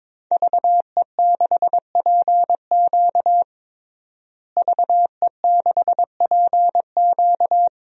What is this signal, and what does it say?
Morse code, heard as a keyed tone, about 22 wpm, 700 Hz. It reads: VE6PQ VE6PQ
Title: DXCC-100 Call Signs - 5 Characters - Repeated 3x - 22wpm